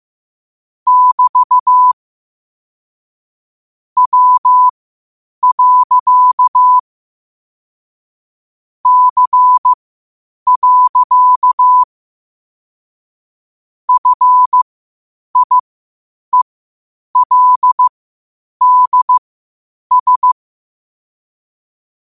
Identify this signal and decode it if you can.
Morse code, heard as a keyed tone, about 15 wpm, spaced as for 8 wpm, 1000 Hz. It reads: <BT> W. C. FIELDS
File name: qotdfiles / 8wpm